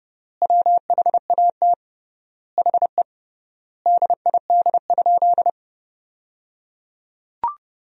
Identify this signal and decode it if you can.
Morse code, heard as a keyed tone, about 30 wpm, 700 Hz. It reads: WHAT HE DID?
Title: 3x Word-Spacing - Sentences from Top 100 Words - 30wpm